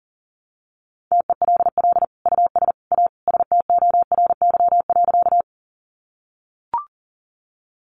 Morse code, heard as a keyed tone, about 40 wpm, 700 Hz. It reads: TELL US A STORY.
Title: Sentences from Top 300 Words 40wpm